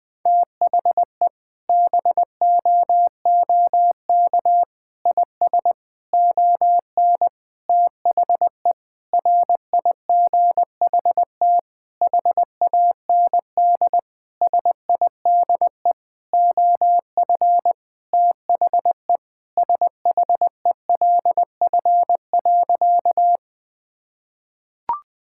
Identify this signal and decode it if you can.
Morse code, heard as a keyed tone, about 20 wpm, 700 Hz. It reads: THE BOOK IS ON THE RIGHT HAND SIDE OF THE SHELF.
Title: Sentences with 7-Word Phrases - 20wpm